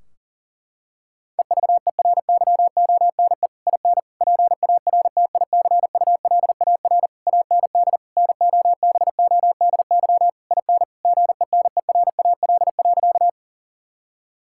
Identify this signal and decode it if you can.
Morse code, heard as a keyed tone, about 40 wpm, 700 Hz. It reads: EVERYONE IN PARTICULAR AND NOBODY IN GENERAL.
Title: Sentences from Top 1600 Words 40wpm